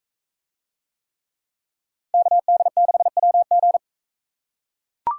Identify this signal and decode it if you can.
Morse code, heard as a keyed tone, about 41 wpm, 700 Hz. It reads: KD6WG